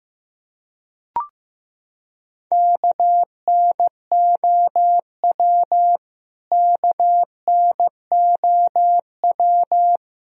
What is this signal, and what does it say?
Morse code, heard as a keyed tone, about 15 wpm, 700 Hz. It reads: KNOW KNOW
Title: Top 100 Words Repeated 15wpm